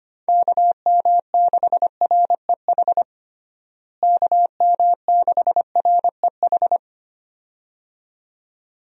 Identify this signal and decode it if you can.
Morse code, heard as a keyed tone, about 25 wpm, 700 Hz. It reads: KM6REH KM6REH
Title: USA Call Signs - 6 Characters - Repeated 2x - 25wpm